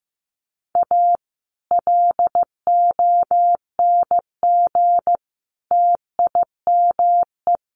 Morse code, heard as a keyed tone, about 15 wpm, 700 Hz. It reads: A LONG TIME
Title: Sets of 3 Words 15wpm